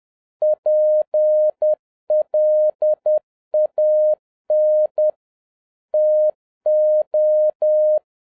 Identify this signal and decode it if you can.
Morse code, heard as a keyed tone, about 10 wpm, 610 Hz. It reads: PLAN TO
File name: 2012-01-mp3 / n09-ITO JR7OEF-10wpm